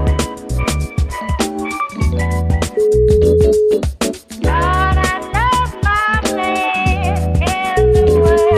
Ringback tone (Europe, 425 Hz), with noise.